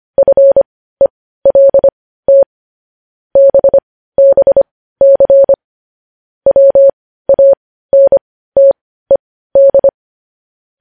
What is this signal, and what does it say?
Morse code, heard as a keyed tone, about 25 wpm, spaced as for 15 wpm, 550 Hz. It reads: FELT BBC WANTED
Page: News headlines in morse code at 15 wpm